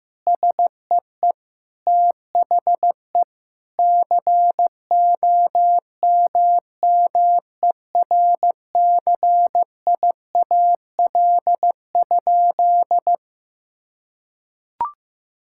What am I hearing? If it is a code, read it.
Morse code, heard as a keyed tone, about 16 wpm, 700 Hz. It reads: SEE THE COMMERCIAL?
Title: Sentences from Top 1000 Words - Review Part 5 - 15wpm